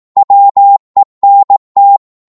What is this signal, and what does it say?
Morse code, heard as a keyed tone, about 18 wpm, 800 Hz. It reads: WENT
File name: Morse / psg-18wpm